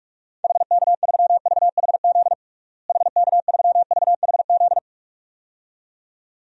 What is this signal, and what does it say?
Morse code, heard as a keyed tone, about 46 wpm, 700 Hz. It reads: HK3VHZ HK3VHZ
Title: DXCC-100 Call Signs - 6 Characters - Repeated 2x - 45wpm